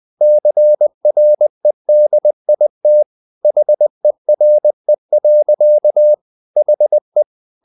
Morse code, heard as a keyed tone, about 20 wpm, 600 Hz. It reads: CREDIT HERE. HE
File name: the old man and the sea / the old man and the sea part 1 20wpm